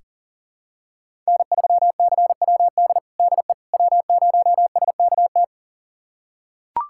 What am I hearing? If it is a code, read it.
Morse code, heard as a keyed tone, about 40 wpm, 700 Hz. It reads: N3CWD DE W0SKT
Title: USA Callsign DE Callsign - 5 Characters - 40wpm